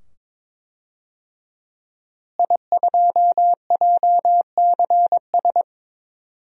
Morse code, heard as a keyed tone, about 22 wpm, 700 Hz. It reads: I2JCS
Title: DXCC-100 Call Signs - 5 Characters - Repeated 2x - 22wpm